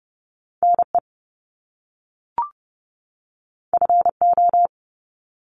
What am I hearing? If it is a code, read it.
Morse code, heard as a keyed tone, about 30 wpm, 700 Hz. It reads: NE FO
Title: Top Two Letter Combinations in Words 30wpm